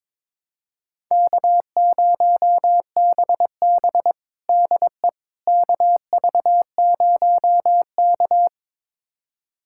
Morse code, heard as a keyed tone, about 22 wpm, 700 Hz. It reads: K0BB DE KV0K
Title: USA Callsign DE Callsign - 4 Characters - 22wpm